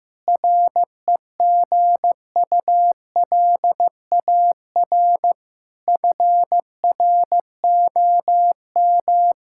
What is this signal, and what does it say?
Morse code, heard as a keyed tone, about 16 wpm, 700 Hz. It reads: REGULAR FROM
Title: Sentences from Top 1000 Words - Review Part 6 - 15wpm